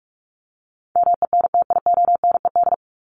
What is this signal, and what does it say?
Morse code, heard as a keyed tone, about 44 wpm, 700 Hz. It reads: MENTIONED
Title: ED Suffix - Top 100 Words with Base Words 45wpm